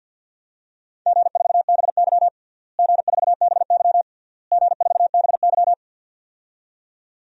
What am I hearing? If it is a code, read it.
Morse code, heard as a keyed tone, about 52 wpm, 700 Hz. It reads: G4BY G4BY G4BY
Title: DXCC-100 Call Signs - 4 Characters - Repeated 3x - 50wpm